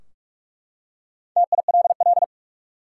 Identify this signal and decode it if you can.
Morse code, heard as a keyed tone, about 43 wpm, 700 Hz. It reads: TILL
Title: Four Letter Words - 45wpm